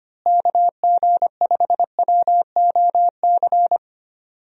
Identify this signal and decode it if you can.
Morse code, heard as a keyed tone, about 25 wpm, 700 Hz. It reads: KG5WOC
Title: USA Call Signs - 6 Characters - Repeated 3x - 25wpm